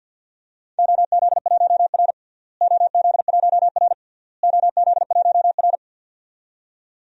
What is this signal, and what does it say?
Morse code, heard as a keyed tone, about 50 wpm, 700 Hz. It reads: OZ1R OZ1R OZ1R
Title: DXCC-100 Call Signs - 4 Characters - Repeated 3x - 50wpm